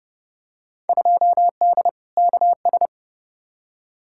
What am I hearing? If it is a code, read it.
Morse code, heard as a keyed tone, about 29 wpm, 700 Hz. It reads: 2D KS
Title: Field Day Exchanges Encore - 30wpm